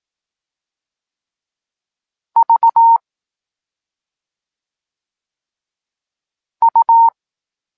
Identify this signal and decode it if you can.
Morse code, heard as a keyed tone, about 18 wpm, spaced as for 2 wpm, 920 Hz. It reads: VU
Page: Tekst 8 – doskonalenie odbioru poznanych liter